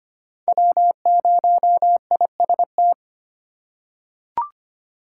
Morse code, heard as a keyed tone, about 25 wpm, 700 Hz. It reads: W0IST
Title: USA Call Signs - 5 Characters - 4x Character Spacing - 25wpm